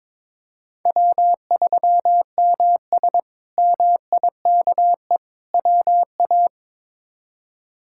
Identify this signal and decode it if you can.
Morse code, heard as a keyed tone, about 22 wpm, 700 Hz. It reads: W3MS MIKE WA
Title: K1USN SST QSO Practice - 22wpm